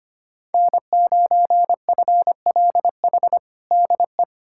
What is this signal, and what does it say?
Morse code, heard as a keyed tone, about 25 wpm, 700 Hz. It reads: N9FLH DE
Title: USA Callsign DE Callsign 25wpm